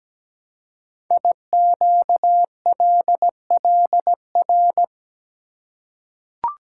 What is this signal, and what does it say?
Morse code, heard as a keyed tone, about 17 wpm, 700 Hz. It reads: IQLLR
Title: Code Groups - 5 Letters - 17wpm